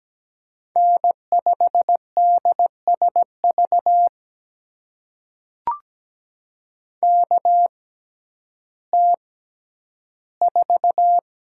Morse code, heard as a keyed tone, about 17 wpm, spaced as for 10 wpm, 700 Hz. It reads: N5DSV KT4
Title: USA Call Signs - 5 Characters - 6x Character Spacing - 17wpm